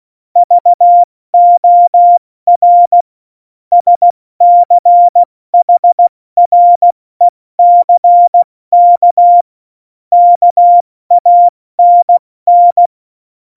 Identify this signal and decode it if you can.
Morse code, heard as a keyed tone, about 16 wpm, 700 Hz. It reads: VOR SCHRECK KANN
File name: Morsekurs / Witze / Witz 26 Mai, 2020 70 F80